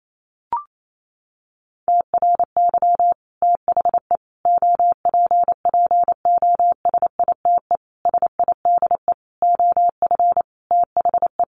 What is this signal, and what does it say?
Morse code, heard as a keyed tone, about 28 wpm, 700 Hz. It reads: TRY THE OPPOSITE SIDE OF THE
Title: Sentences from Top 1300 Words 28wpm